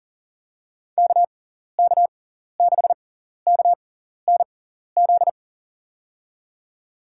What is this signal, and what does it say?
Morse code, heard as a keyed tone, about 40 wpm, spaced as for 14 wpm, 700 Hz. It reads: KK6KNZ
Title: USA Call Signs - 6 Characters - 6x Character Spacing - 40wpm